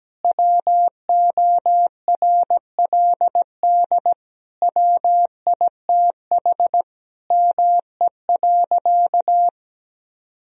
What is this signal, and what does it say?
Morse code, heard as a keyed tone, about 17 wpm, 700 Hz. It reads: WORLD WITH ME.
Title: Sentences from Top 1000 Words - Review Part 3 - 17wpm